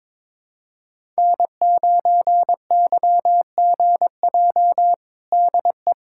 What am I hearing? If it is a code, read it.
Morse code, heard as a keyed tone, about 22 wpm, 700 Hz. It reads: N9YGJ DE
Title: USA Callsign DE Callsign 22wpm